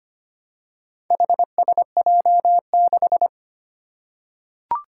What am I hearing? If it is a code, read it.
Morse code, heard as a keyed tone, about 25 wpm, 700 Hz. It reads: HSJ6
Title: Four Challenging Letter-Number Combinations 25wpm